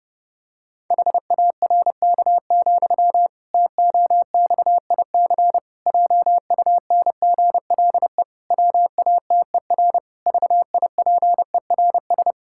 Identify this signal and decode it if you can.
Morse code, heard as a keyed tone, about 30 wpm, 700 Hz. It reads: HARK, TOXIC JUNGLE WATER VIPERS
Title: Pangram Challenge v2 - 30wpm